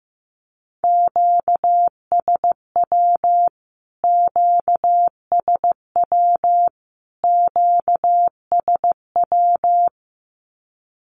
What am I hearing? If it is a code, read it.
Morse code, heard as a keyed tone, about 15 wpm, 700 Hz. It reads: QSW QSW QSW
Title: Q-Codes Repeated 3x 15wpm